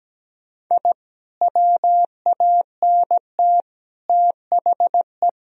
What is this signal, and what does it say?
Morse code, heard as a keyed tone, about 17 wpm, 700 Hz. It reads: I WANT THE
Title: Sentences from Top 1000 Words - Review Part 1 - 17wpm